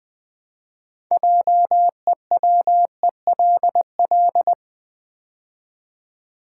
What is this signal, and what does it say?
Morse code, heard as a keyed tone, about 20 wpm, 700 Hz. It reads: JEWELL
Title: Build A Word - 3 Letters At A Time - 9-Letter Words - 20wpm